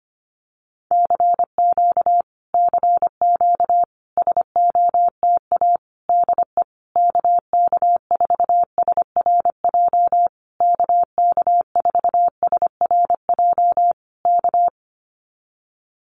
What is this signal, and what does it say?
Morse code, heard as a keyed tone, about 25 wpm, 700 Hz. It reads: CQ CQ SOTA DE KK4SRJ KK4SRJ K
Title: Common QSO Phrases Encore 25wpm